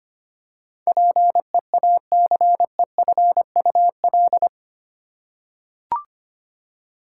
Morse code, heard as a keyed tone, about 25 wpm, 700 Hz. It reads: PEACEFUL
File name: 8 Letter Words - First 3 Letters Given - 25wpm